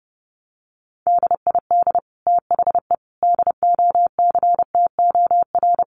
Morse code, heard as a keyed tone, about 30 wpm, 700 Hz. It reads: DID THE DOCTOR
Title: Sentences from Top 1500 Words 30wpm